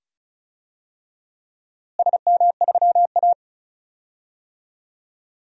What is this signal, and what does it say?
Morse code, heard as a keyed tone, about 36 wpm, 700 Hz. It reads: SM3A
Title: DXCC-100 Call Signs - 4 Characters - Repeated 2x - 35wpm